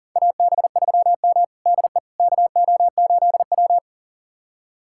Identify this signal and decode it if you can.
Morse code, heard as a keyed tone, about 40 wpm, 700 Hz. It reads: AB3M DE KO8W
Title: USA Callsign DE Callsign - 4 Characters - 40wpm